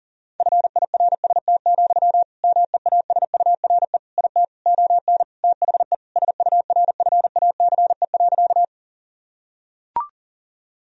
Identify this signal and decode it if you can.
Morse code, heard as a keyed tone, about 37 wpm, 700 Hz. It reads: FIRST, MEASURE IT ON THE SURFACE.
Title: Sentences from Top 1000 Words - Review Part 3 - 40wpm